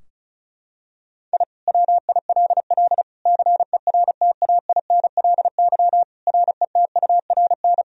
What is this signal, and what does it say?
Morse code, heard as a keyed tone, about 36 wpm, 700 Hz. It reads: I WILL CERTAINLY RETURN
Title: Sentences from Top 1000 Words - Review Part 4 - 35wpm